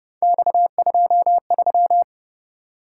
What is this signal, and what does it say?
Morse code, heard as a keyed tone, about 29 wpm, 700 Hz. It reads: X23
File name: Speed-Racing - Three Letter-Number Combinations - 25wpm